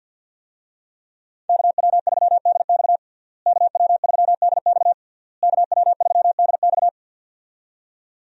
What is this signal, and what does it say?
Morse code, heard as a keyed tone, about 47 wpm, 700 Hz. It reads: KW3DX KW3DX KW3DX
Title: USA Call Signs - 5 Characters - Repeated 3x - 50wpm